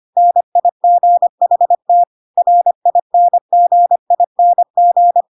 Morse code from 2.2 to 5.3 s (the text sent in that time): RINGING